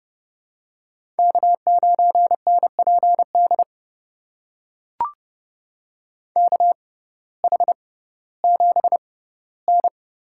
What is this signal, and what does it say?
Morse code, heard as a keyed tone, about 32 wpm, spaced as for 17 wpm, 700 Hz. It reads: K9NPD KH7N
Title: USA Call Signs - 5 Characters - 6x Character Spacing - 30wpm